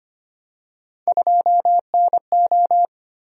Morse code, heard as a keyed tone, about 25 wpm, 700 Hz. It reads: 2NO
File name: Three Letter-Number Combinations - 25wpm